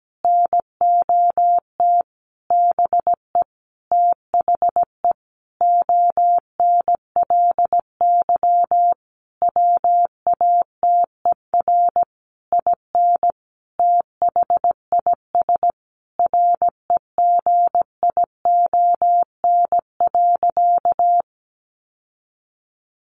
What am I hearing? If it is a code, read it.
Morse code, heard as a keyed tone, about 17 wpm, 700 Hz. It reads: NOT BE THE ONLY WATER IN THIS REGION.
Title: Sentences from Top 1500 Words 17wpm